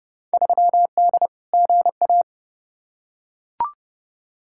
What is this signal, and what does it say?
Morse code, heard as a keyed tone, about 30 wpm, 700 Hz. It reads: 3D GA
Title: Field Day Exchanges Encore - 30wpm